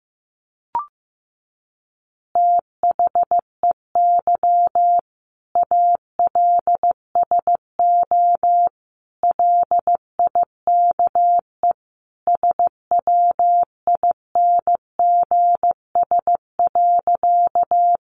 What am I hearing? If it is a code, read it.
Morse code, heard as a keyed tone, about 14 wpm, 700 Hz. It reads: THEY ALSO LIKE SWINGS.
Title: Easy Phrases 15wpm